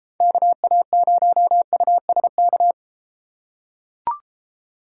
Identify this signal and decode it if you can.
Morse code, heard as a keyed tone, about 33 wpm, 700 Hz. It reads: KA0USK